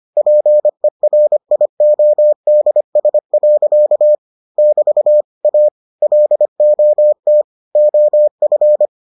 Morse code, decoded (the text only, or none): PERIODS. <BT> A LOT OF